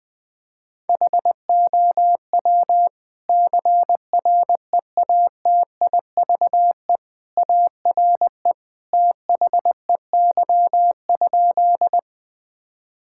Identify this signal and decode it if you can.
Morse code, heard as a keyed tone, about 20 wpm, 700 Hz. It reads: HOW CREATIVE ARE THEY?
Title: Sentences from Top 1000 Words - Review Part 5 - 20wpm